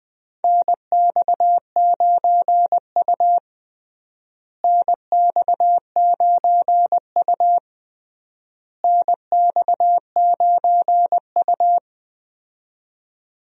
Morse code, heard as a keyed tone, about 20 wpm, 700 Hz. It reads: NX9U NX9U NX9U
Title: USA Call Signs - 4 Characters - Repeated 3x - 20wpm